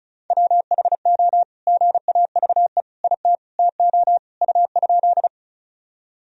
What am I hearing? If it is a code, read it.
Morse code, heard as a keyed tone, about 34 wpm, 700 Hz. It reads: WHO GAVE IT TO U?
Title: Sentences from Top 500 Words and Abbreviations 35wpm